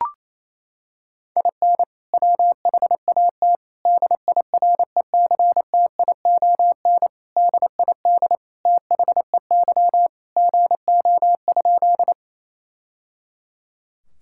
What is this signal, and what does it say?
Morse code, heard as a keyed tone, about 28 wpm, 700 Hz. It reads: IN WHAT DIRECTION DID THEY GO?
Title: Sentences from Top 1000 Words - Review Part 5 - 28wpm